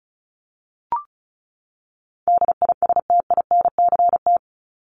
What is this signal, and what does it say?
Morse code, heard as a keyed tone, about 34 wpm, 700 Hz. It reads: DISTINCT
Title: Top 1501-1750 Words 35wpm